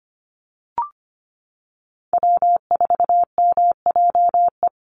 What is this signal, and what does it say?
Morse code, heard as a keyed tone, about 25 wpm, 700 Hz. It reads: W4MJE
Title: Call Signs Repeated 3x - 25wpm